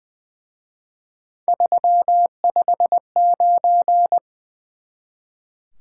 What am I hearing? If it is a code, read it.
Morse code, heard as a keyed tone, about 24 wpm, 700 Hz. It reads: 359
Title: Speed-Racing Top 180 QSO Elements 20wpm